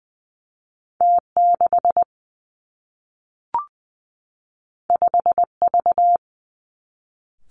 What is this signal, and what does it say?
Morse code, heard as a keyed tone, about 20 wpm, 700 Hz. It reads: T 6 5 V
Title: Two Challenging Letter-Number Combinations 20wpm